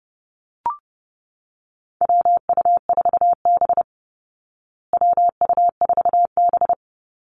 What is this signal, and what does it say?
Morse code, heard as a keyed tone, about 30 wpm, 700 Hz. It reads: WU4B WU4B
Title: Call Signs Repeated 2x - 30wpm